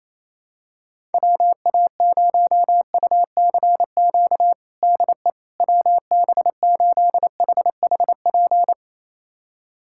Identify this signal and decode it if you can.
Morse code, heard as a keyed tone, about 27 wpm, 700 Hz. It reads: WA0UCQ DE WB8HHP
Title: USA Callsign DE Callsign - 6 Characters - 28wpm